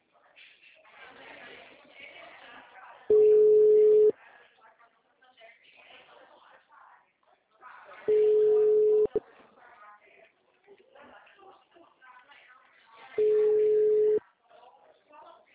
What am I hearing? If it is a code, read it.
Ringback tone (Europe, 425 Hz), with speech.